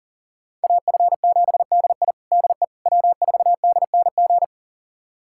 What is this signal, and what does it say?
Morse code, heard as a keyed tone, about 40 wpm, 700 Hz. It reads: AF7DI DE W4DNG
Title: USA Callsign DE Callsign - 5 Characters - 40wpm